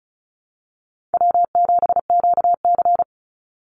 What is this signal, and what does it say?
Morse code, heard as a keyed tone, about 34 wpm, 700 Hz. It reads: W7QC
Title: Easy Call Signs Repeated 2x - 35wpm